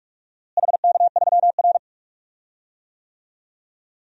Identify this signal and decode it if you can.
Morse code, heard as a keyed tone, about 46 wpm, 700 Hz. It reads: HK3R
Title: DXCC-100 Call Signs - 4 Characters - Repeated 3x - 45wpm